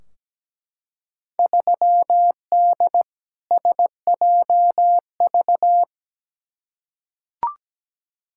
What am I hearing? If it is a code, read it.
Morse code, heard as a keyed tone, about 17 wpm, 700 Hz. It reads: 3D SJV
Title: Field Day Exchanges - 17wpm